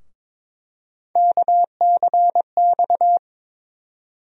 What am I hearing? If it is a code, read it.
Morse code, heard as a keyed tone, about 21 wpm, 700 Hz. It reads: KCX
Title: USA Call Sign Postfixes - 3 Letters 22wpm